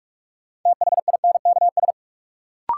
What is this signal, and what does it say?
Morse code, heard as a keyed tone, about 46 wpm, 700 Hz. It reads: THINKS
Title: Sentences from Top 1000 Words - Review Part 6 - New Word Drill - 45wpm